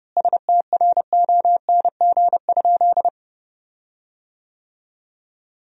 Morse code, heard as a keyed tone, about 30 wpm, 700 Hz. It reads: STRONG?
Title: 4x Word-Spacing - Sentences from Top 500 Words - 30wpm